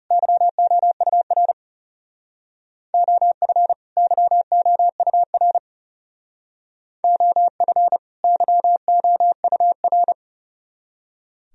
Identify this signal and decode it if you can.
Morse code, heard as a keyed tone, about 36 wpm, 700 Hz. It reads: YOUR OF YOUR OF YOUR
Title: Speed-Racing - Sets of 2 Words 30wpm